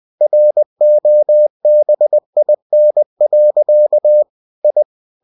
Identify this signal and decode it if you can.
Morse code, heard as a keyed tone, about 20 wpm, 600 Hz. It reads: ROBIN. I
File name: cw-books / winnie-the-pooh / winnie-20wpm-page-0033